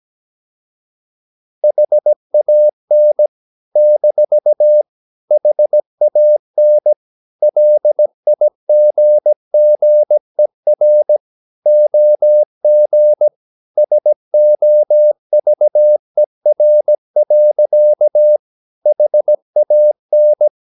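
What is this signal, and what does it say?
Morse code, heard as a keyed tone, about 17 wpm, 600 Hz. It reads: HAN - HAN LIGGER OG SOVER. HAN
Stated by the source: Slow (QRS) Morse practice at 17 wpm, from CW med Gnister.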